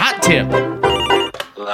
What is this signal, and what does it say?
Speech. There is no code.